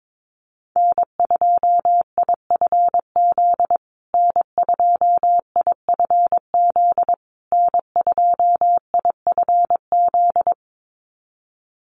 Morse code, heard as a keyed tone, about 22 wpm, 700 Hz. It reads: N2IFZ N2IFZ N2IFZ
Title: USA Call Signs Repeated 3x 22wpm